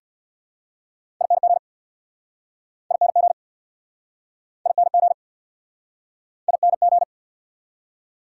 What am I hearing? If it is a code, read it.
Morse code, heard as a keyed tone, about 69 wpm, 700 Hz. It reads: ING ING ING ING